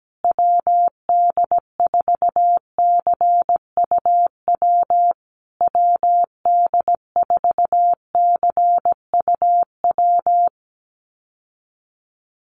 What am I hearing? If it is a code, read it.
Morse code, heard as a keyed tone, about 17 wpm, 700 Hz. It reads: WD4CUW WD4CUW
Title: USA Call Signs Repeated 3x 17wpm